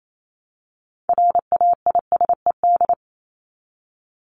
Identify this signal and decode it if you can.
Morse code, heard as a keyed tone, about 28 wpm, 700 Hz. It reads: RAISED
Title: ED Suffix - Top 100 Words with Base Words 28wpm